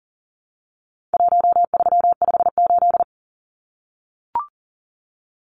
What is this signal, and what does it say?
Morse code, heard as a keyed tone, about 40 wpm, 700 Hz. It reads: 1358
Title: Numbers - Contest Style - 4-Digit 40wpm